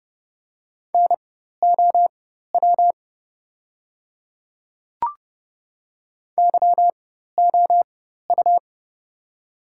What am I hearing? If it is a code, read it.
Morse code, heard as a keyed tone, about 29 wpm, spaced as for 14 wpm, 700 Hz. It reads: NOW YOU
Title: 4x Character Spacing - Top 200 Words - 30wpm